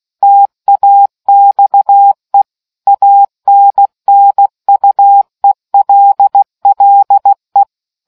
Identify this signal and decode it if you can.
Morse code, heard as a keyed tone, about 16 wpm, 800 Hz. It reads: TAXE ANNUELLE